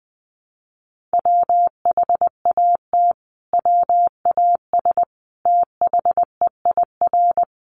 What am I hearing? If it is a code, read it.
Morse code, heard as a keyed tone, about 20 wpm, 700 Hz. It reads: WHAT WAS THEIR
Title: Sentences from Top 1400 Words 20wpm